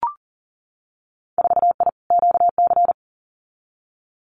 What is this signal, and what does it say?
Morse code, heard as a keyed tone, about 38 wpm, 700 Hz. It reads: 4I QC
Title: Winter Field Day Exchanges Encore 40wpm